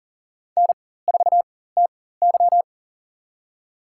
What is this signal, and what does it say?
Morse code, heard as a keyed tone, about 40 wpm, spaced as for 19 wpm, 700 Hz. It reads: N4TY
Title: USA Call Signs - 4x Character Spacing - 40wpm